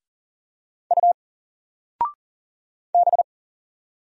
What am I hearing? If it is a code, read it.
Morse code, heard as a keyed tone, about 40 wpm, 700 Hz. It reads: U B
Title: Single Letter-Number 40wpm